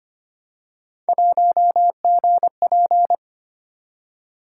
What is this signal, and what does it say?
Morse code, heard as a keyed tone, about 25 wpm, 700 Hz. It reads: 1GP
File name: Three Letter-Number Combinations - 25wpm